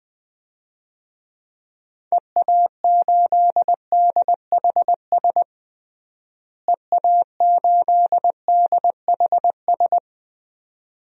Morse code, heard as a keyed tone, about 21 wpm, 700 Hz. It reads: EA8DHS EA8DHS
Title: DXCC-100 Call Signs - 6 Characters - Repeated 2x - 20wpm